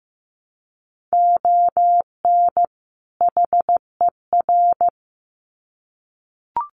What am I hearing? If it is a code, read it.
Morse code, heard as a keyed tone, about 15 wpm, 700 Hz. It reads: ON HER
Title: Sets of 2 Words 15wpm